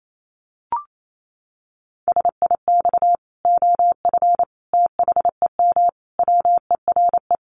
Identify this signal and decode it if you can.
Morse code, heard as a keyed tone, about 28 wpm, 700 Hz. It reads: SIX OF THEM WERE
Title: Sentences from Top 1100 Words 28wpm